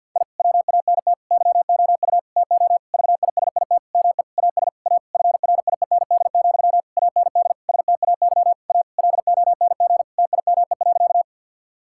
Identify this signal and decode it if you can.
Morse code, heard as a keyed tone, about 50 wpm, 700 Hz. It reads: I WANT YOU TO VISIT ME AS A FRIEND, AND STAY A LONG TIME.
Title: Sentences from Top 1000 Words - Review Part 6 - 50wpm